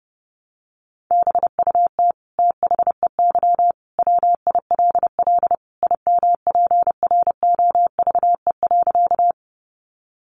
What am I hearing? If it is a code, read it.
Morse code, heard as a keyed tone, about 30 wpm, 700 Hz. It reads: BUT THEY WILL IMPROVE.
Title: Sentences from Top 800 Words 30wpm